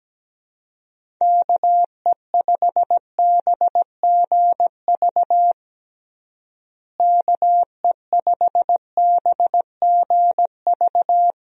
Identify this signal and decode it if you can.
Morse code, heard as a keyed tone, about 17 wpm, 700 Hz. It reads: KE5BGV KE5BGV
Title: USA Call Signs - 6 Characters - Repeated 2x - 17wpm